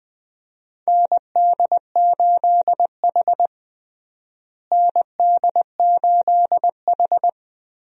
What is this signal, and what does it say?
Morse code, heard as a keyed tone, about 20 wpm, 700 Hz. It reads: ND8H ND8H
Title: USA Call Signs - 4 Characters - Repeated 3x - 20wpm